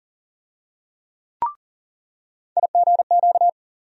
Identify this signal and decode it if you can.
Morse code, heard as a keyed tone, about 40 wpm, 700 Hz. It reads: IGQ
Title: USA Call Sign Postfixes - 3 Letters 40wpm